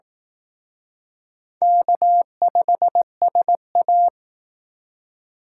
Morse code, heard as a keyed tone, about 19 wpm, 700 Hz. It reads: K5SA